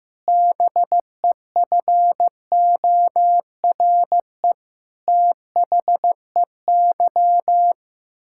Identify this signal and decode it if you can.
Morse code, heard as a keyed tone, about 15 wpm, 700 Hz. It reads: BEFORE THEY
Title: Pangram Challenge v2 - 15wpm